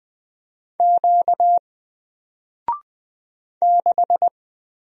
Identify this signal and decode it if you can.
Morse code, heard as a keyed tone, about 20 wpm, 700 Hz. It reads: Q 6
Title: Single Letter-Number 20wpm